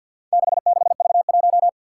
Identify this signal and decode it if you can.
Morse code, heard as a keyed tone, about 46 wpm, 700 Hz. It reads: 66V1
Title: Four Challenging Letter-Number Combinations 50wpm